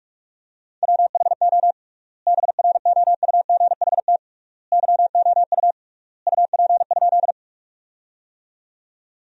Morse code, heard as a keyed tone, about 41 wpm, 700 Hz. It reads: WHO BROUGHT YOU UP?
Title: 3x Word-Spacing - Sentences from Top 500 Words - 45wpm